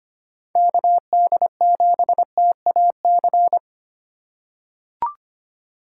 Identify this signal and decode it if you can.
Morse code, heard as a keyed tone, about 25 wpm, 700 Hz. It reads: KD7TAC
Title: USA Call Signs - 6 Characters - 8x Character Spacing - 25wpm